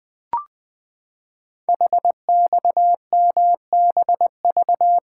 Morse code, heard as a keyed tone, about 20 wpm, 700 Hz. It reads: HXMBV
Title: Code Groups - 5 Letters - 20wpm